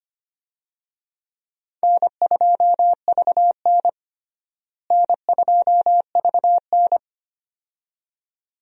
Morse code, heard as a keyed tone, about 25 wpm, 700 Hz. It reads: N2VN N2VN
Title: USA Call Signs - 4 Characters - Repeated 2x - 25wpm